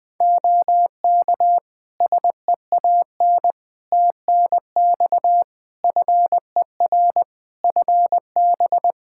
Morse code, heard as a keyed tone, about 20 wpm, 700 Hz. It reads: OK SEAN TNX FER FB
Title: Practice QSO - Exchange 4 - Ending - 20wpm